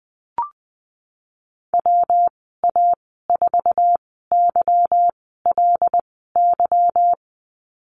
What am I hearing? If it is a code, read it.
Morse code, heard as a keyed tone, about 19 wpm, spaced as for 14 wpm, 700 Hz. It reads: WA4YLY
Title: 147 Lesson 24g - Letter y - Call Signs 20wpm